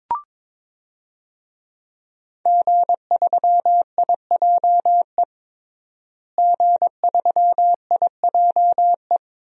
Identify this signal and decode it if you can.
Morse code, heard as a keyed tone, about 22 wpm, 700 Hz. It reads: G3IJE G3IJE
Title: DXCC-100 Call Signs - 5 Characters - Repeated 2x - 22wpm